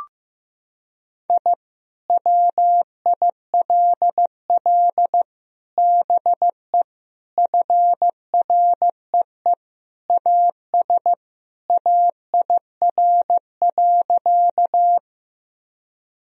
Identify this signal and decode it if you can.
Morse code, heard as a keyed tone, about 15 wpm, 700 Hz. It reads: I WILL BE FREE AS AIR.
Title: Sentences from Top 1000 Words - Review Part 1 - 15wpm